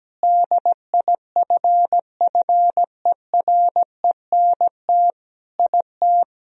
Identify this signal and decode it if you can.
Morse code, heard as a keyed tone, about 17 wpm, 700 Hz. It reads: DIFFERENT IT
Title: Sentences from Top 1000 Words - Review Part 3 - 17wpm